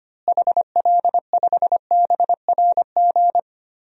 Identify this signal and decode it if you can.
Morse code, heard as a keyed tone, about 25 wpm, 700 Hz. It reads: HL5BRG
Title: DXCC-100 Call Signs - 6 Characters - Repeated 3x - 25wpm